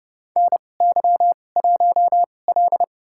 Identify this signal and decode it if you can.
Morse code, heard as a keyed tone, about 30 wpm, spaced as for 22 wpm, 700 Hz. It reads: NY1L
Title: USA Call Signs - 4 Characters - 2x Character Spacing - 30wpm